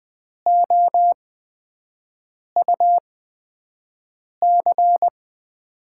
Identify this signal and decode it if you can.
Morse code, heard as a keyed tone, about 20 wpm, spaced as for 5 wpm, 700 Hz. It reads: OUC
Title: USA Call Signs - 8x Character Spacing - 20wpm